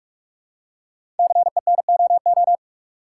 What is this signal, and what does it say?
Morse code, heard as a keyed tone, about 44 wpm, 700 Hz. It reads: CENOO
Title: Code Groups - 5 Letters - 45wpm